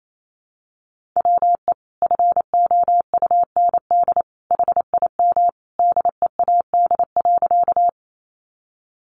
Morse code, heard as a keyed tone, about 28 wpm, 700 Hz. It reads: WE FOUND HIM DEAD.
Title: Sentences from Top 1500 Words 28wpm